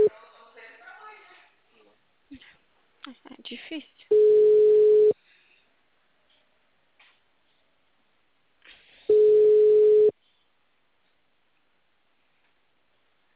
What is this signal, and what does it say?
Ringback tone (Europe, 425 Hz), with speech.
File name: out- -2055- - - .133